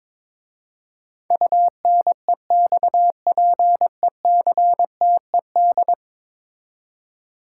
Morse code, heard as a keyed tone, about 22 wpm, 700 Hz. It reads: UNEXPECTED
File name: practice / 22wpm / 10 Letter Words - First 3 Letters Given - 22wpm